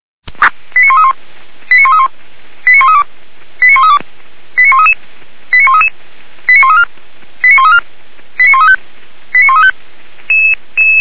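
Five-tone selective call (ZVEI), heard as quick runs of five tones, tones of about 80 ms. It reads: 89131 89131 89132 89132 89133 89133 89134 89134 89135 89135
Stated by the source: ZVEI five-tone selective call received with an SDR.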